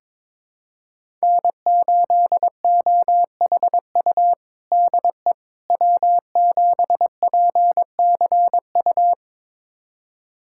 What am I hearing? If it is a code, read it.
Morse code, heard as a keyed tone, about 22 wpm, 700 Hz. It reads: N8OHU DE W7PCU
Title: USA Callsign DE Callsign - 5 Characters - 22wpm